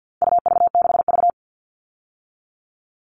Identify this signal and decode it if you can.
Morse code, heard as a keyed tone, about 50 wpm, 700 Hz. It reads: UV6V
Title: All Cut Numbers - 4-Digit 50wpm